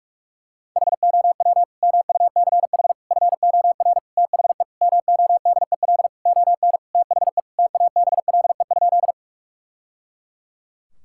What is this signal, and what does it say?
Morse code, heard as a keyed tone, about 46 wpm, 700 Hz. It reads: HOW MUCH FOR THE MODEL ON THE TABLE?
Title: Sentences from Top 1000 Words - Review Part 5 - 45wpm